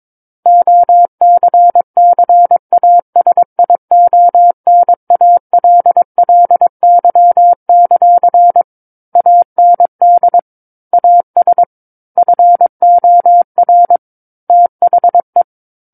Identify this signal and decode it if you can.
Morse code, heard as a keyed tone, about 22 wpm, 700 Hz. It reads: OCCASIONALLY; AND AS FOR THE
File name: 06-Chapter-06 22 WPM